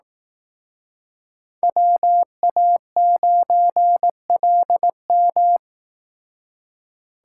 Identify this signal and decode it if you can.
Morse code, heard as a keyed tone, about 18 wpm, 700 Hz. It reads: WA9LM
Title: Speed Racing - USA 2x2 Call Signs 15wpm